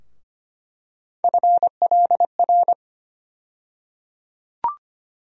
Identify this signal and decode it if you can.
Morse code, heard as a keyed tone, about 25 wpm, 700 Hz. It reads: FLR
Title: USA Call Sign Postfixes - 3 Letters 25wpm